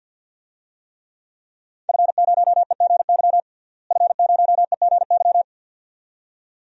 Morse code, heard as a keyed tone, about 53 wpm, 700 Hz. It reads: F0EGY F0EGY
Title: DXCC-100 Call Signs - 5 Characters - Repeated 2x - 50wpm